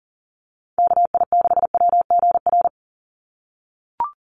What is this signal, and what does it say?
Morse code, heard as a keyed tone, about 40 wpm, 700 Hz. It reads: KI6WGR
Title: USA Call Signs 40wpm